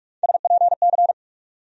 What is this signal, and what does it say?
Morse code, heard as a keyed tone, about 47 wpm, 700 Hz. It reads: SPC
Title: USA Call Sign Postfixes - 3 Letters 45wpm